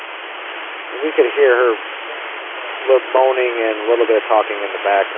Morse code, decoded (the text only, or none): none